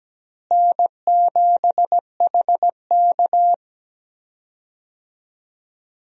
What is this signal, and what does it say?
Morse code, heard as a keyed tone, about 17 wpm, 700 Hz. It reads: N7HK
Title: USA Call Signs - 4 Characters - Repeated 2x - 17wpm